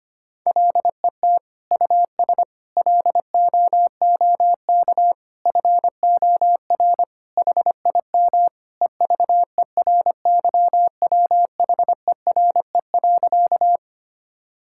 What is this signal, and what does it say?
Morse code, heard as a keyed tone, about 25 wpm, 700 Hz. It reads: LET US LOOK FOR HIM EVERYWHERE.
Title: Sentences from Top 1600 Words 25wpm